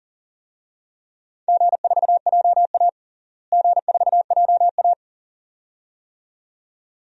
Morse code, heard as a keyed tone, about 40 wpm, 700 Hz. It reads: G4JA G4JA
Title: DXCC-100 Call Signs - 4 Characters - Repeated 2x - 40wpm